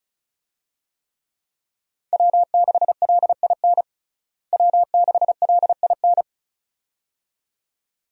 Morse code, heard as a keyed tone, about 36 wpm, 700 Hz. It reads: W6LIN W6LIN
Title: USA Call Signs - 5 Characters - Repeated 2x - 35wpm